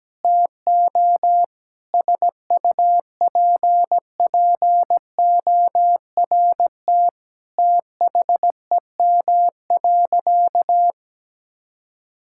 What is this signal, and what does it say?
Morse code, heard as a keyed tone, about 17 wpm, 700 Hz. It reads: TO SUPPORT THEM.
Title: Sentences from Top 1000 Words - Review Part 5 - 17wpm